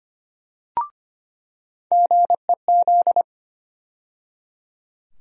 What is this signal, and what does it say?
Morse code, heard as a keyed tone, about 25 wpm, 700 Hz. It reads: GEZ
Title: USA Call Sign Postfixes - 3 Letters 25wpm